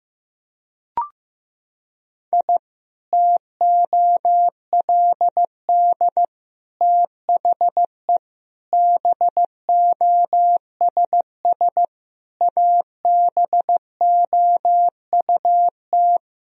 Morse code, heard as a keyed tone, about 15 wpm, 700 Hz. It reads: I TOLD THE BOSS ABOUT
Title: Sentences from Top 1000 Words - Review Part 4 - 15wpm